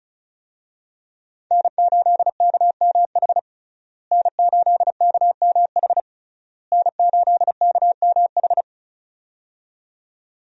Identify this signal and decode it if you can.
Morse code, heard as a keyed tone, about 36 wpm, 700 Hz. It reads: N8KMH N8KMH N8KMH
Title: USA Call Signs - 5 Characters - Repeated 3x - 35wpm